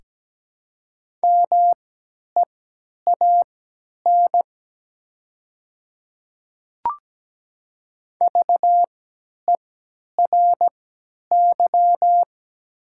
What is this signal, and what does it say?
Morse code, heard as a keyed tone, about 17 wpm, spaced as for 10 wpm, 700 Hz. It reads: MEAN VERY
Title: 3x Character Spacing - Top 200 Words - 17wpm